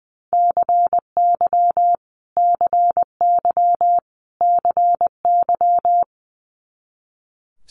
Morse code, heard as a keyed tone, about 20 wpm, 700 Hz. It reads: CY CY CY
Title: Multi-letter Phonemes Repeated 3x 20wpm